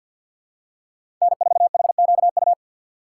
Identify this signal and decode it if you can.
Morse code, heard as a keyed tone, about 47 wpm, 700 Hz. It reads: N4HQU
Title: USA Call Signs - 5 Characters - 6x Character Spacing - 50wpm